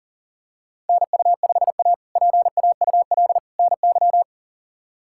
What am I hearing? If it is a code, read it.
Morse code, heard as a keyed tone, about 40 wpm, 700 Hz. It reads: NU5A PAUL NY
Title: K1USN SST QSO Practice - 40wpm